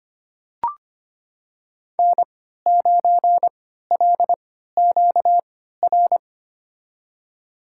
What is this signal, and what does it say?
Morse code, heard as a keyed tone, about 25 wpm, spaced as for 14 wpm, 700 Hz. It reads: N9LQR
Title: USA Call Signs - 3x Character Spacing - 25wpm